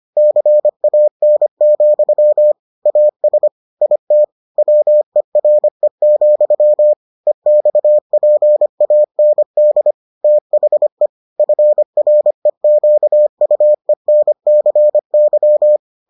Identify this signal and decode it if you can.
Morse code, heard as a keyed tone, about 25 wpm, 600 Hz. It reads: CAN, AS IT WERE, EXPAND THE FREQUENCY